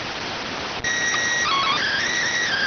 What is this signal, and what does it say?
SSTV (slow-scan television), PD 290.